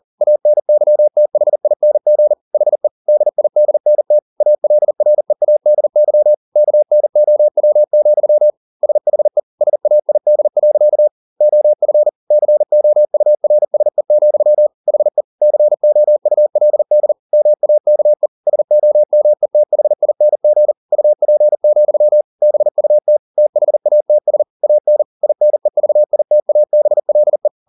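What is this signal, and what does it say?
Morse code, heard as a keyed tone, about 40 wpm, 600 Hz. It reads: ANYTHING HE DIDNT ALREADY KNOW, SHE SAID. OF COURSE, HE COULD MAKE SOMETHING UP, BUT THATS AN INEVITABLE
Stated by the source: Morse code at 40 wpm, practice text, CW med Gnister.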